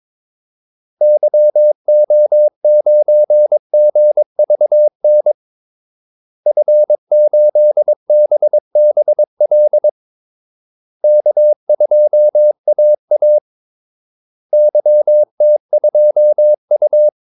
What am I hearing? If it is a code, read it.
Morse code, heard as a keyed tone, about 22 wpm, 600 Hz. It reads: YO9GVN F8BBL K2AA YT2U
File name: Callsigns-22wpm